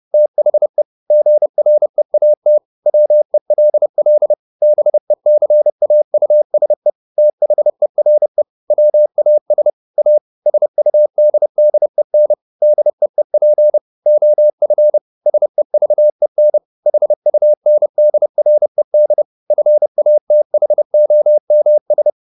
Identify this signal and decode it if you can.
Morse code, heard as a keyed tone, about 30 wpm, 600 Hz. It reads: THE GREAT WELL BECAUSE THERE WAS A SUDDEN DEEP OF SEVEN HUNDRED FATHOMS
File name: the old man and the sea / the old man and the sea 1 30wpm